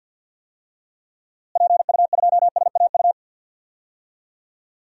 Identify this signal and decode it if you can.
Morse code, heard as a keyed tone, about 52 wpm, 700 Hz. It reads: PU2SAU
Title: DXCC-100 Call Signs - 6 Characters - Repeated 2x - 50wpm